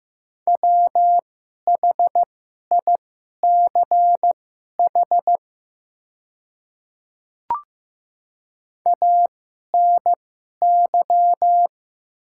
Morse code, heard as a keyed tone, about 15 wpm, spaced as for 11 wpm, 700 Hz. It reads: WHICH ANY
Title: 2x Character Spacing - Top 300 Words - 15wpm